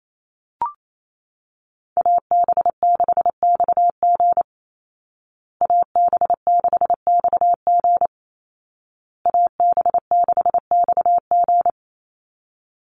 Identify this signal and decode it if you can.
Morse code, heard as a keyed tone, about 28 wpm, 700 Hz. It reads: AB6XG AB6XG AB6XG
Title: Call Signs Repeated 2x - 28wpm